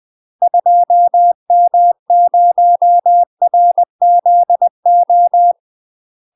Morse code, heard as a keyed tone, about 20 wpm, 700 Hz. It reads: 2M0RZO
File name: UK MASTER QSOs TRAINING / 20WPM / UK MASTER QSOs 0063